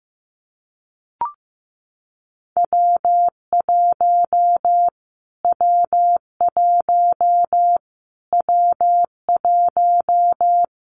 Morse code, heard as a keyed tone, about 15 wpm, 700 Hz. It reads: W1 W1 W1
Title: Callsign Prefixes for Top 25 DXCC - Repeated 3x - 15wpm